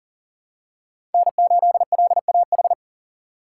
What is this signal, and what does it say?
Morse code, heard as a keyed tone, about 40 wpm, 700 Hz. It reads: N8LAH
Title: USA Call Signs - 5 Characters - 3x Character Spacing - 40wpm